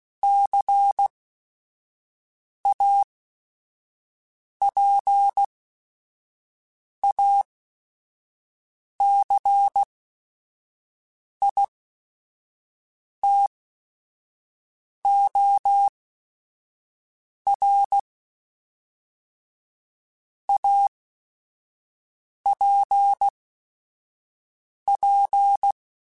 Morse code, heard as a keyed tone, about 16 wpm, spaced as for 5 wpm, 800 Hz. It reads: CAPACITOR APP
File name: Technical Subjects / Technical Subjects-mp3s / XC @5